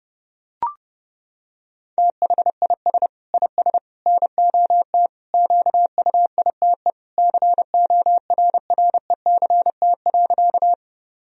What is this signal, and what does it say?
Morse code, heard as a keyed tone, about 30 wpm, 700 Hz. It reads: THIS IS NOT QUITE CORRECT.
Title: Sentences from Top 1000 Words - Review Part 2 - 30wpm